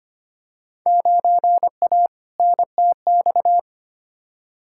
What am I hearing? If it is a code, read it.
Morse code, heard as a keyed tone, about 25 wpm, 700 Hz. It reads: 9A NTX
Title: Field Day Exchanges Encore - 25wpm